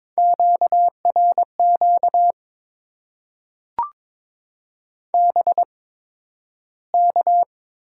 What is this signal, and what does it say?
Morse code, heard as a keyed tone, about 22 wpm, spaced as for 12 wpm, 690 Hz. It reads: QRQ BK
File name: Top 90 QSO Elements - 8x Character Spacing - 22wpm